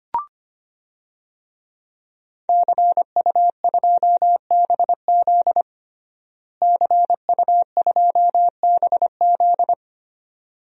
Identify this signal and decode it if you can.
Morse code, heard as a keyed tone, about 25 wpm, 700 Hz. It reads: CU2BZ CU2BZ
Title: DXCC-100 Call Signs - 5 Characters - Repeated 2x - 25wpm